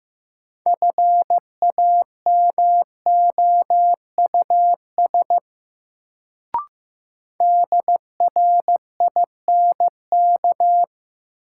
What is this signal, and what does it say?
Morse code, heard as a keyed tone, about 15 wpm, 700 Hz. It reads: FAMOUS DRINK
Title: Sentences from Top 1500 Words - New Word Drill - 15wpm